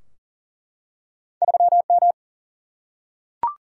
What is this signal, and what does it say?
Morse code, heard as a keyed tone, about 42 wpm, 700 Hz. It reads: 3 M
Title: Single-Letter Single-Number 40wpm